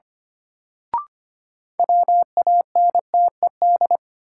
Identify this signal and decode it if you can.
Morse code, heard as a keyed tone, about 25 wpm, 700 Hz. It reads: WANTED
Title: Sets of 2 Words - New Word Drill - 25wpm